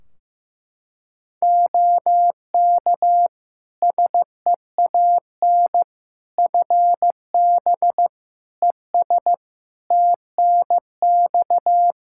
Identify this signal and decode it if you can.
Morse code, heard as a keyed tone, about 15 wpm, 700 Hz. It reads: OK SEAN FB ES TNX
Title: Practice QSO - Exchange 2 - Info - 15wpm